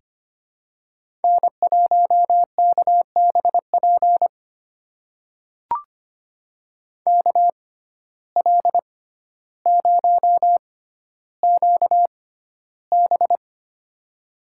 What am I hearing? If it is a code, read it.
Morse code, heard as a keyed tone, about 25 wpm, spaced as for 13 wpm, 700 Hz. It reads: N1KBP KL0QB
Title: USA Call Signs - 5 Characters - 6x Character Spacing - 25wpm